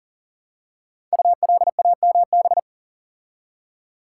Morse code, heard as a keyed tone, about 38 wpm, 700 Hz. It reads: ULAMB
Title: Code Groups - 5 Letters - 40wpm